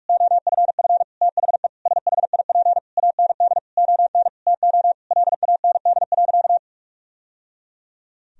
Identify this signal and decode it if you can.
Morse code, heard as a keyed tone, about 46 wpm, 700 Hz. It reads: OFF THE SHIP AND ON TO LAND.
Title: Sentences from Top 1000 Words - Review Part 2 - 45wpm